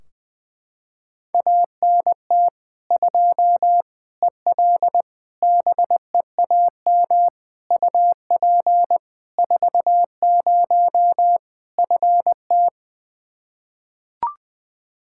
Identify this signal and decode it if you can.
Morse code, heard as a keyed tone, about 21 wpm, 700 Hz. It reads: ANT 2 EL BEAM UP 40 FT
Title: Practice QSO - Exchange 2 - Info - 20wpm